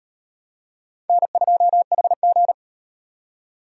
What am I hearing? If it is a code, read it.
Morse code, heard as a keyed tone, about 38 wpm, 700 Hz. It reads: N2HG